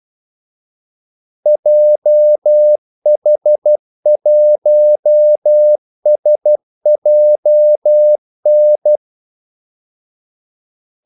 Morse code, heard as a keyed tone, about 12 wpm, 600 Hz. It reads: JH1SJN
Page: 2 ekstra ordmellemrum mellem kaldesignalerne